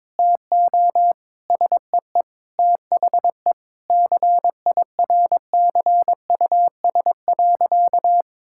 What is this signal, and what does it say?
Morse code, heard as a keyed tone, about 22 wpm, 700 Hz. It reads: TO SEE THE CIRCUS.
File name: Difficult Sentences with Hints - 22wpm